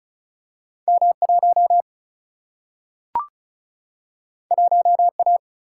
Morse code, heard as a keyed tone, about 36 wpm, 700 Hz. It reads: M 1 1 A
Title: Single-Letter Single-Number 35wpm